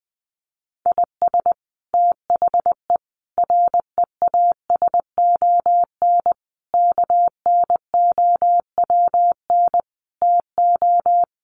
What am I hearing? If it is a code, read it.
Morse code, heard as a keyed tone, about 20 wpm, 700 Hz. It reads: IS THE REASON KNOWN TO
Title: Sentences from Top 700 Words 20wpm